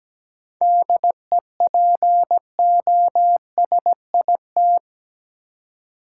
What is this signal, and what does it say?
Morse code, heard as a keyed tone, about 17 wpm, 700 Hz. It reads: DEPOSIT
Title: Seven Letter Words - 17wpm